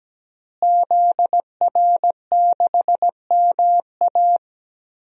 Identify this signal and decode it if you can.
Morse code, heard as a keyed tone, about 17 wpm, 700 Hz. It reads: ZR6MA
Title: DXCC-100 Call Signs - 5 Characters - Repeated 3x - 17wpm